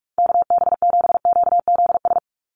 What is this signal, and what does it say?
Morse code, heard as a keyed tone, about 44 wpm, 700 Hz. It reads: KB7QZS
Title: USA Call Signs 45wpm